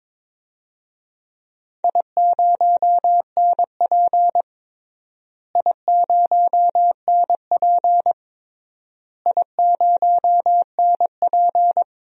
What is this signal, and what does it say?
Morse code, heard as a keyed tone, about 22 wpm, 700 Hz. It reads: I0NP I0NP I0NP
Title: DXCC-100 Call Signs - 4 Characters - Repeated 3x - 22wpm